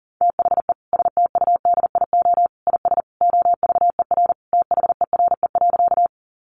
Morse code, heard as a keyed tone, about 37 wpm, 700 Hz. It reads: THE STUDIO IS OVER THERE.
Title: Sentences from Top 1200 Words 40wpm